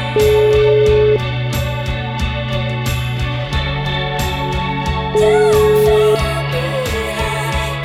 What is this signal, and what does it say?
Ringback tone (Europe, 425 Hz), with noise.